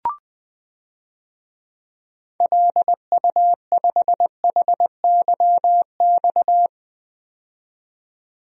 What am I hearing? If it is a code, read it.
Morse code, heard as a keyed tone, about 20 wpm, 700 Hz. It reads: LU5HYX
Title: DXCC-100 Call Signs - 6 Characters - 20wpm